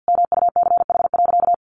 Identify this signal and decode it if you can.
Morse code, heard as a keyed tone, about 50 wpm, 700 Hz. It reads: MUCH.
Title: Sentences from Top 500 Words 50wpm